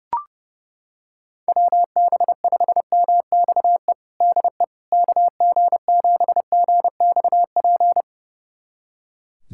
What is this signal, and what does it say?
Morse code, heard as a keyed tone, about 30 wpm, 700 Hz. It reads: WB5MXE DE KG7GXP
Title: USA Callsign DE Callsign - 6 Characters - 30wpm